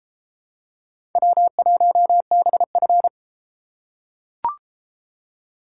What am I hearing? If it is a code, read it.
Morse code, heard as a keyed tone, about 30 wpm, 700 Hz. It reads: W1BF